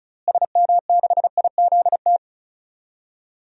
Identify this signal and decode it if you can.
Morse code, heard as a keyed tone, about 36 wpm, 700 Hz. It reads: SM6IZT
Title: DXCC-100 Call Signs - 6 Characters - Repeated 3x - 35wpm